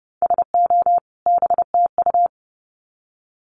Morse code, heard as a keyed tone, about 30 wpm, 700 Hz. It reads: SO BTU
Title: Common QSO Phrases 30wpm